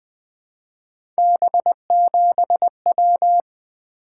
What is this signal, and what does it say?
Morse code, heard as a keyed tone, about 20 wpm, 700 Hz. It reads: B7W
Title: Three Letter-Number Combinations - 20wpm